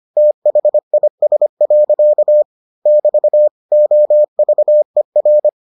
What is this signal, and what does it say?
Morse code, heard as a keyed tone, about 25 wpm, 600 Hz. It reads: THIS. <BT> OVER